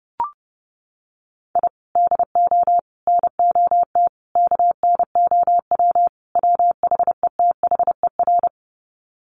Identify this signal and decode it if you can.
Morse code, heard as a keyed tone, about 30 wpm, 700 Hz. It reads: I DO NOT KNOW WHETHER
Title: Sets of 5 Words 30wpm